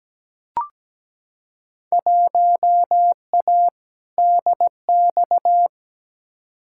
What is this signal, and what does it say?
Morse code, heard as a keyed tone, about 17 wpm, 700 Hz. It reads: 1A DX
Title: Field Day Exchanges Encore - 17wpm